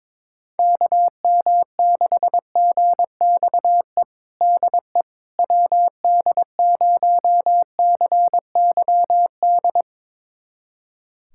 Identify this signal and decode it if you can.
Morse code, heard as a keyed tone, about 22 wpm, 700 Hz. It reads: KM6GXE DE WD0CYD
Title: USA Callsign DE Callsign - 6 Characters - 22wpm